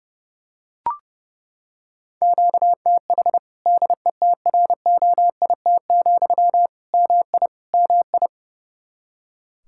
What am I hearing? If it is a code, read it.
Morse code, heard as a keyed tone, about 30 wpm, 700 Hz. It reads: QTH DETROIT, MI MI
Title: Practice QSO - Exchange 1 - Beginning - 30wpm